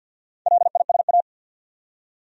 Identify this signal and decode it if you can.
Morse code, heard as a keyed tone, about 48 wpm, 700 Hz. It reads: LISA
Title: USA Common Names - Max 5 Letters - 50wpm